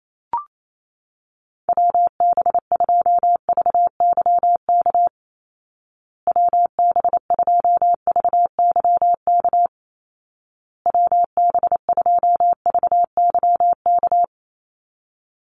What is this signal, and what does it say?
Morse code, heard as a keyed tone, about 28 wpm, 700 Hz. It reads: WB2VYK WB2VYK WB2VYK
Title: Call Signs Repeated 2x - 28wpm